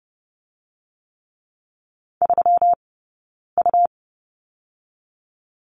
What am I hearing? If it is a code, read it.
Morse code, heard as a keyed tone, about 30 wpm, spaced as for 9 wpm, 700 Hz. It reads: 3U
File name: Lesson 20d - Number 3 - Two Characters 30wpm